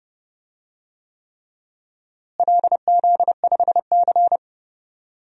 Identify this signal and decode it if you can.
Morse code, heard as a keyed tone, about 31 wpm, 700 Hz. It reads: LZ5C
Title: DXCC-100 Call Signs - 4 Characters - Repeated 3x - 30wpm